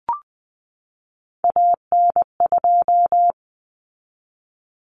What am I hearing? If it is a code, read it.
Morse code, heard as a keyed tone, about 19 wpm, 700 Hz. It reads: AN2
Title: Common Cut Numbers - 3-Digit 20wpm